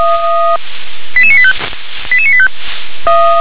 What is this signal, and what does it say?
Five-tone selective call (ZVEI), heard as quick runs of five tones, tones of about 70 ms. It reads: 88985 88985
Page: /357/SDR/ZVEI/